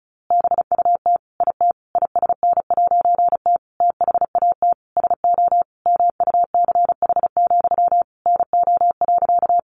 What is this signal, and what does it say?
Morse code, heard as a keyed tone, about 34 wpm, 700 Hz. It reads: BUT IT ISN'T THAT SO MUCH, NO.
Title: A Sound Of Thunder 35wpm